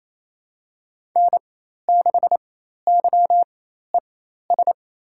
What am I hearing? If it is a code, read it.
Morse code, heard as a keyed tone, about 27 wpm, spaced as for 13 wpm, 700 Hz. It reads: N6YES
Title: USA Call Signs - 5 Characters - 4x Character Spacing - 28wpm